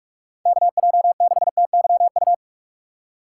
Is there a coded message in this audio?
Yes: Morse code from 0.5 to 2.3 s, heard as a keyed tone, about 57 wpm, 700 Hz. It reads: KJ6TYU